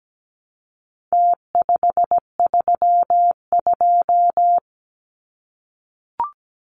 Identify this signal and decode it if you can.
Morse code, heard as a keyed tone, about 17 wpm, 700 Hz. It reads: T532
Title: Numbers - Contest Style - 4-Digit 17wpm